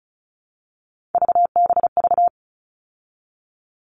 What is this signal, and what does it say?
Morse code, heard as a keyed tone, about 34 wpm, 700 Hz. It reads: VBV
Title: All Cut Numbers - 3-Digit 35wpm